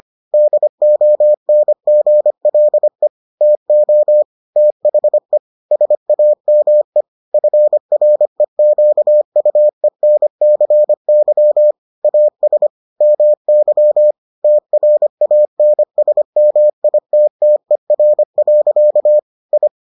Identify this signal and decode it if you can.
Morse code, heard as a keyed tone, about 25 wpm, 600 Hz. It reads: DONGLE TO THE SAME FREQUENCY AS MY TRANSMITTER. I